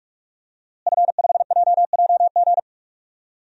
Morse code, heard as a keyed tone, about 47 wpm, 700 Hz. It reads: F5JJG
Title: DXCC-100 Call Signs - 5 Characters - Repeated 2x - 45wpm